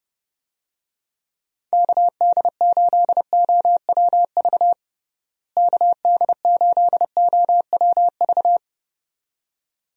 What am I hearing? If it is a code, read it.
Morse code, heard as a keyed tone, about 30 wpm, 700 Hz. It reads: KD8OWV KD8OWV
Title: USA Call Signs - 6 Characters - Repeated 2x - 30wpm